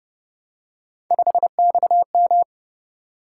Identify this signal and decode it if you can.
Morse code, heard as a keyed tone, about 28 wpm, 700 Hz. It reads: 5XM